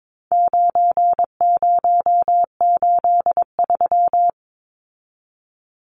Morse code, heard as a keyed tone, about 22 wpm, 700 Hz. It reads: 9083
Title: Numbers - 4-Digit 22wpm